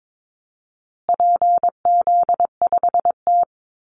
Morse code, heard as a keyed tone, about 22 wpm, 700 Hz. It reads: PZ5T
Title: Call Signs Repeated 2x - 22wpm